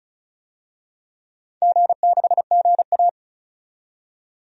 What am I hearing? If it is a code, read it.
Morse code, heard as a keyed tone, about 36 wpm, 700 Hz. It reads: G6GA
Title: DXCC-100 Call Signs - 4 Characters - 35wpm